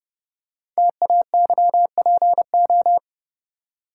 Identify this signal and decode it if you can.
Morse code, heard as a keyed tone, about 30 wpm, 700 Hz. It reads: TAYPO
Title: Code Groups - 5 Letters - 30wpm